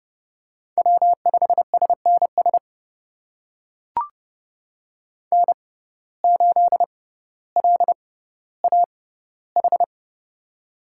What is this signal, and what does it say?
Morse code, heard as a keyed tone, about 30 wpm, spaced as for 15 wpm, 700 Hz. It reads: W5SNS N8LAH
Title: USA Call Signs - 5 Characters - 6x Character Spacing - 30wpm